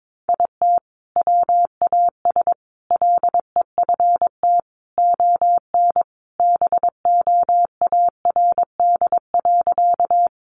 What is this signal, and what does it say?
Morse code, heard as a keyed tone, about 22 wpm, 700 Hz. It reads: IT WAS LEFT ON BOARD.
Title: Sentences from Top 700 Words 22wpm